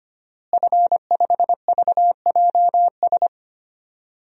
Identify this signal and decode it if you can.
Morse code, heard as a keyed tone, about 25 wpm, 700 Hz. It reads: F5VJS
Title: DXCC-100 Call Signs - 5 Characters - Repeated 2x - 25wpm